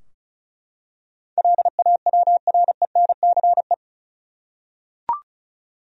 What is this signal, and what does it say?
Morse code, heard as a keyed tone, about 34 wpm, 700 Hz. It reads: LAWRENCE
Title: USA Common Names - 35wpm